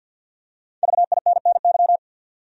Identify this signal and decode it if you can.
Morse code, heard as a keyed tone, about 49 wpm, 700 Hz. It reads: VINNY
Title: USA Common Names - Max 5 Letters - 50wpm